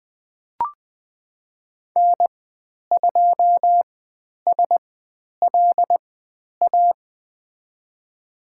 Morse code, heard as a keyed tone, about 20 wpm, spaced as for 10 wpm, 700 Hz. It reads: N2SLA
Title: 095 Lesson 16g - Letter c - Call Signs 20wpm